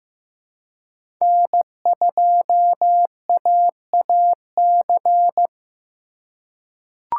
Morse code, heard as a keyed tone, about 15 wpm, 700 Hz. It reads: N2AAC
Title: USA Call Signs - 4x Character Spacing - 15wpm